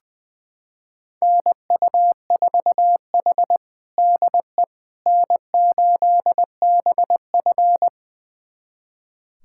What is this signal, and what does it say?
Morse code, heard as a keyed tone, about 20 wpm, 700 Hz. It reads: NU4H DE N8BF
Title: USA Callsign DE Callsign - 4 Characters - 20wpm